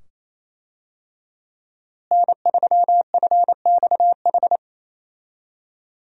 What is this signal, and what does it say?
Morse code, heard as a keyed tone, about 28 wpm, 700 Hz. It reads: N3FXH
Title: USA Call Signs - 5 Characters - Repeated 2x - 28wpm